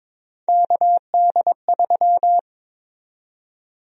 Morse code, heard as a keyed tone, about 22 wpm, 700 Hz. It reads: KD3
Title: USA Call Sign Prefixes - 22wpm